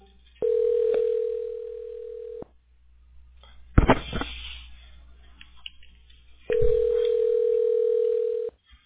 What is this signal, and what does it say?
Ringback tone (North America), 440 + 480 Hz.